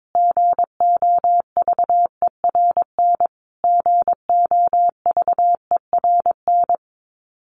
Morse code, heard as a keyed tone, about 22 wpm, 700 Hz. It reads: GOVERN GOVERN
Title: Top 500 Words Repeated 22wpm